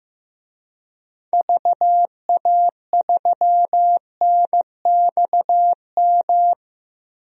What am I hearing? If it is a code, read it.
Morse code, heard as a keyed tone, about 16 wpm, 700 Hz. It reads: VA3NXM
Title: DXCC-100 Call Signs - 6 Characters - Repeated 2x - 15wpm